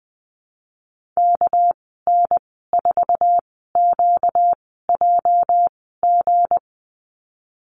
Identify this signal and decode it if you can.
Morse code, heard as a keyed tone, about 20 wpm, spaced as for 15 wpm, 700 Hz. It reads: KN4QJG
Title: 245 Lesson 39g - Letter z - Call Signs 20wpm